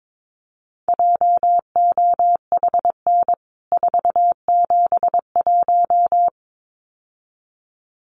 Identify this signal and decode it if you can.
Morse code, heard as a keyed tone, about 22 wpm, 700 Hz. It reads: JOHN 471
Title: CWT Exchange 22wpm